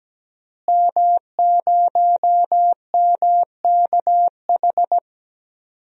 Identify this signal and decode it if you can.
Morse code, heard as a keyed tone, about 17 wpm, 700 Hz. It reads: M0MKH
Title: DXCC-100 Call Signs - 5 Characters - Repeated 2x - 17wpm